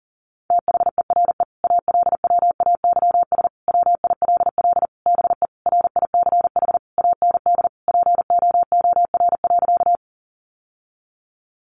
Morse code, heard as a keyed tone, about 40 wpm, 700 Hz. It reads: THERE ALWAYS WILL BE RICH AND POOR.
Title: Sentences from Top 1100 Words 40wpm